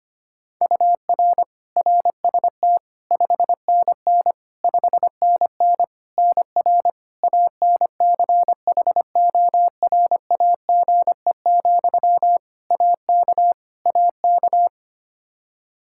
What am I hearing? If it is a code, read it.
Morse code, heard as a keyed tone, about 25 wpm, 700 Hz. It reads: UR RST 5NN 5NN NR ANCHORAGE, AK AK
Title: Practice QSO - Exchange 1 - Beginning - 25wpm